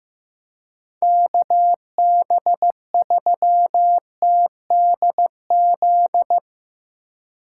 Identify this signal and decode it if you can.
Morse code, heard as a keyed tone, about 15 wpm, 700 Hz. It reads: KB3TDZ
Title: USA Call Signs - 6 Characters - 8x Character Spacing - 15wpm